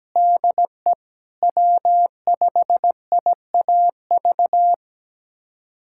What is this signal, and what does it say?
Morse code, heard as a keyed tone, about 17 wpm, 700 Hz. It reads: DE W5IAV
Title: USA Callsign DE Callsign 17wpm